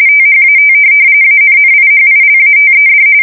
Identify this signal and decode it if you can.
RTTY, about 45.45 baud. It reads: THAT ANNIHILATED
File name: fsk170 / USS-New-Jersey